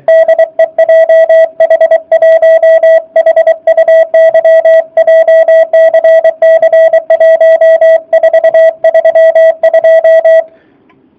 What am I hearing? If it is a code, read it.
Morse code, heard as a keyed tone, about 24 wpm, 660 Hz. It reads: DE JH1HUY JCC 1432